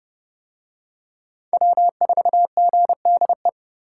Morse code, heard as a keyed tone, about 26 wpm, 700 Hz. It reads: W4GDE